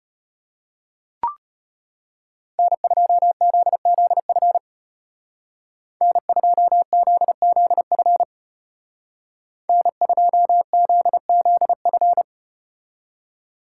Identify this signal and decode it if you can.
Morse code, heard as a keyed tone, about 35 wpm, 700 Hz. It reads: N2ZZF N2ZZF N2ZZF